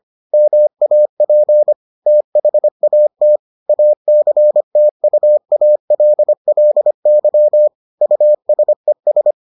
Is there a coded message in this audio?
Yes: Morse code from 0.3 to 9.3 s, heard as a keyed tone, about 25 wpm, 600 Hz. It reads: MAP THAT ACTUALLY USES